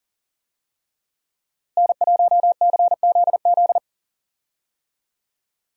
Morse code, heard as a keyed tone, about 39 wpm, 700 Hz. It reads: N1CZZ
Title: USA Call Signs - 5 Characters - Repeated 3x - 40wpm